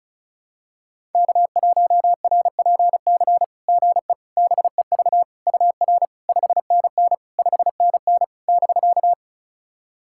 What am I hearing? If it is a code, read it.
Morse code, heard as a keyed tone, about 34 wpm, 690 Hz. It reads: K1RPC GE BEV UR 5NN 5NN <BK>
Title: SOTA QSO Practice 35wpm